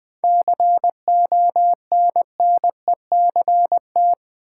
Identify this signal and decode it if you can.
Morse code, heard as a keyed tone, about 20 wpm, 700 Hz. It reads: CONNECT
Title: Sentences from Top 1500 Words - New Word Drill Alt - 20wpm